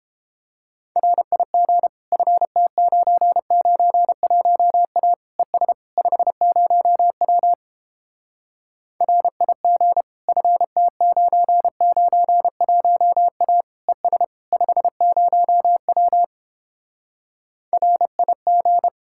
Morse code, heard as a keyed tone, about 29 wpm, 700 Hz. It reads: RIG FT991A ES 50W RIG FT991A ES 50W RIG